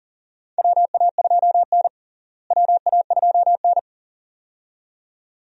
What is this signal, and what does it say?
Morse code, heard as a keyed tone, about 40 wpm, 700 Hz. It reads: WA2N WA2N
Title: USA Call Signs - 4 Characters - Repeated 2x - 40wpm